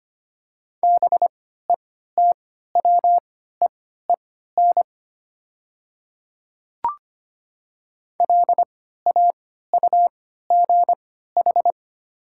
Morse code, heard as a keyed tone, about 25 wpm, spaced as for 14 wpm, 700 Hz. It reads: BETWEEN LAUGH
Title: 3x Character Spacing - Top 500 Words - 25wpm